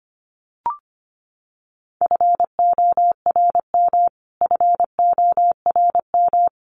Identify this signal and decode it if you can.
Morse code, heard as a keyed tone, about 25 wpm, 700 Hz. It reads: FORM FORM
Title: Top 200 Words Repeated 25wpm